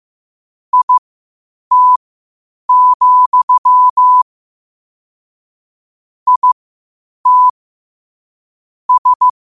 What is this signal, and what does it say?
Morse code, heard as a keyed tone, about 15 wpm, spaced as for 8 wpm, 1000 Hz. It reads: IT, IT S
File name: qotdfiles / 8wpm